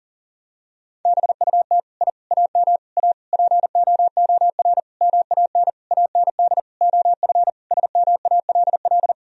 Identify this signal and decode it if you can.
Morse code, heard as a keyed tone, about 40 wpm, 700 Hz. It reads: BUT I AM A POOR MAN AND OF SMALL
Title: Sentences from Top 1000 Words - Review Part 3 - 40wpm